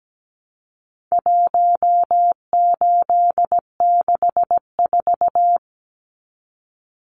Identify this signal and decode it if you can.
Morse code, heard as a keyed tone, about 17 wpm, 700 Hz. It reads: 1864
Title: Numbers - 4-Digit 17wpm